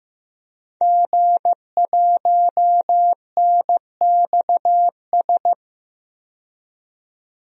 Morse code, heard as a keyed tone, about 15 wpm, 700 Hz. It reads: G1NXS
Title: DXCC-100 Call Signs - 5 Characters - 15wpm